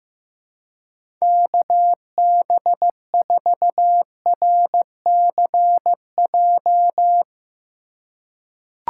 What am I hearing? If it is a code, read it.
Morse code, heard as a keyed tone, about 16 wpm, 700 Hz. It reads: KB4RCJ
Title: USA Call Signs - 6 Characters - 2x Character Spacing - 15wpm